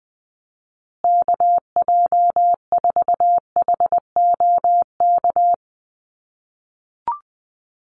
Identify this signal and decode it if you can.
Morse code, heard as a keyed tone, about 20 wpm, 700 Hz. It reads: KJ4HOK
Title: USA Call Signs Repeated 2x 20wpm